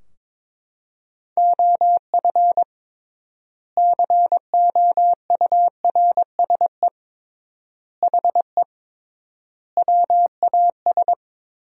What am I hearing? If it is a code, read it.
Morse code, heard as a keyed tone, about 23 wpm, 700 Hz. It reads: OF COURSE HE WAS
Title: 3x Word-Spacing - Sentences from Top 500 Words - 22wpm